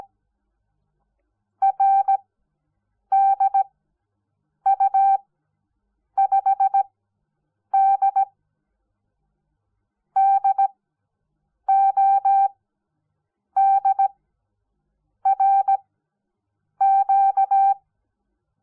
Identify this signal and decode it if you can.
Morse code, heard as a keyed tone, about 17 wpm, spaced as for 7 wpm, 780 Hz. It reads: RDU5D DODRQ
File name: Onlinelehrgang / morsen / DJ4UF Morsen05